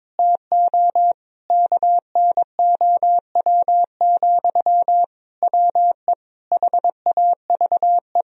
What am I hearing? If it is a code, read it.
Morse code, heard as a keyed tone, about 22 wpm, 700 Hz. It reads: TO KNOW, WE HAVE
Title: Sentences from Top 1000 Words - Review Part 3 - 22wpm